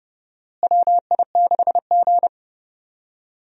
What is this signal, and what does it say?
Morse code, heard as a keyed tone, about 30 wpm, 700 Hz. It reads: WI6G
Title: USA Call Signs - 4 Characters - Repeated 3x - 30wpm